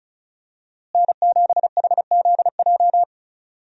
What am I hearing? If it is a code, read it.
Morse code, heard as a keyed tone, about 36 wpm, 700 Hz. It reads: N7HZJ
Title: USA Call Signs - 5 Characters - Repeated 3x - 35wpm